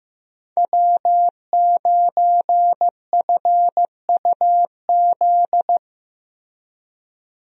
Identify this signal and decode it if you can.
Morse code, heard as a keyed tone, about 15 wpm, 700 Hz. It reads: W9FUZ
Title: USA Call Signs - 5 Characters - 15wpm